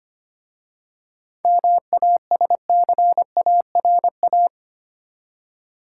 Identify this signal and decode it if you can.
Morse code, heard as a keyed tone, about 25 wpm, 700 Hz. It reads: MASCARA
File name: practice / 25wpm / Easy Words with Hints - 25wpm